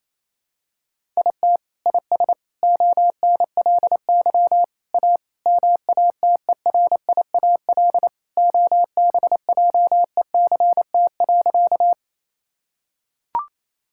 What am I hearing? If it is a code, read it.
Morse code, heard as a keyed tone, about 28 wpm, 700 Hz. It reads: IT IS ONLY A MATERIAL OBJECT.
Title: Sentences from Top 1000 Words - Review Part 5 - 28wpm